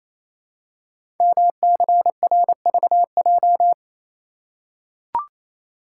Morse code, heard as a keyed tone, about 28 wpm, 690 Hz. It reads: MCRVJ
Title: Code Groups - 5 Letters - 28wpm